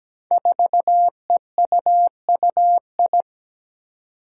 Morse code, heard as a keyed tone, about 17 wpm, 700 Hz. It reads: 4EUUI
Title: Five Challenging Letter-Number Combinations 17wpm